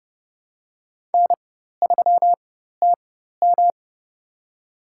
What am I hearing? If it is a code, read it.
Morse code, heard as a keyed tone, about 30 wpm, spaced as for 14 wpm, 700 Hz. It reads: N3TM
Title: USA Call Signs - 4 Characters - 4x Character Spacing - 30wpm